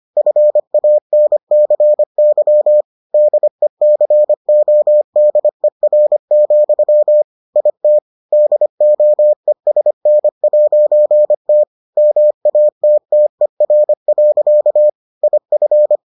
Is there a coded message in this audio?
Yes: Morse code from 0.2 to 16.0 s, heard as a keyed tone, about 25 wpm, 600 Hz. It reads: FANCY DECODER, IT DOESN'T MATTER. IF